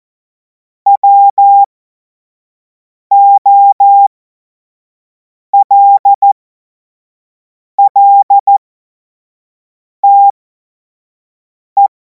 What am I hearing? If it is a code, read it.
Morse code, heard as a keyed tone, about 14 wpm, spaced as for 5 wpm, 800 Hz. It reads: WOLLTE
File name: Morsekurs / Witze / Witz 12. Mai 2026 25 F70